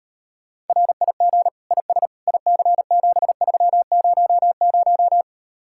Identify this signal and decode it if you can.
Morse code, heard as a keyed tone, about 38 wpm, 700 Hz. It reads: RIG IS IC7300